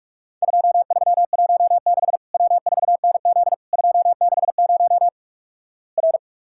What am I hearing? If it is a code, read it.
Morse code, heard as a keyed tone, about 44 wpm, 700 Hz. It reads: 2316 W4NZ 260 R
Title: Worked All Europe DX Contest - QTC Exchange - 45wpm